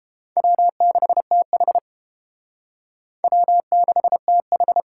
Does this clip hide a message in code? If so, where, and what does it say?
Yes: Morse code from 0.4 to 4.8 s, heard as a keyed tone, about 30 wpm, 700 Hz. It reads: W6TH W6TH